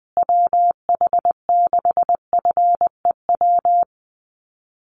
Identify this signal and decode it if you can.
Morse code, heard as a keyed tone, about 20 wpm, 700 Hz. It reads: WH6FEW
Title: USA Call Signs Repeated 3x 20wpm